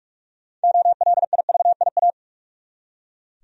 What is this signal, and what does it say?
Morse code, heard as a keyed tone, about 45 wpm, 700 Hz. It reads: OLIVIA
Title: USA Common Names - 45wpm